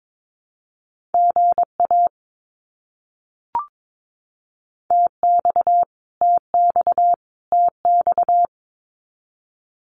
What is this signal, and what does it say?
Morse code, heard as a keyed tone, about 22 wpm, 700 Hz. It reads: GA TX TX TX
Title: Abbreviations Repeated 3x - 22wpm